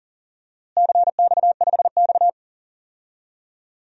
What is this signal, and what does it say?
Morse code, heard as a keyed tone, about 40 wpm, 700 Hz. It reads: CX5X
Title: DXCC-100 Call Signs - 4 Characters - Repeated 2x - 40wpm